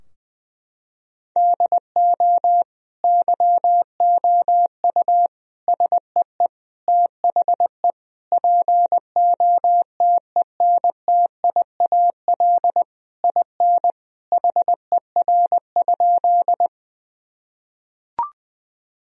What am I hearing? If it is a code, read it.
Morse code, heard as a keyed tone, about 20 wpm, 700 Hz. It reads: DO YOU SEE THE POTENTIAL IN HER?
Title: Sentences from Top 1000 Words - Review Part 4 - 20wpm